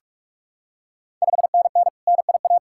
Morse code, heard as a keyed tone, about 46 wpm, 690 Hz. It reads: 5NN NIA
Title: Ontario QSO Party Exchange 45wpm